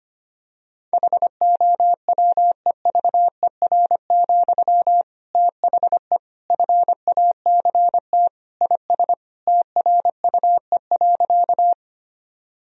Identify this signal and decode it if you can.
Morse code, heard as a keyed tone, about 25 wpm, 700 Hz. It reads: HOWEVER, THE FACT IS TRUE.
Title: Sentences from Top 1000 Words - Review Part 3 - 25wpm